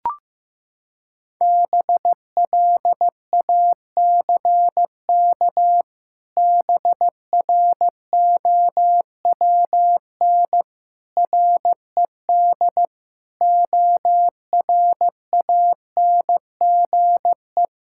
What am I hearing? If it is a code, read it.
Morse code, heard as a keyed tone, about 15 wpm, 700 Hz. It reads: BLACK BROWN RED ORANGE
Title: Pangram Challenge v2 - 15wpm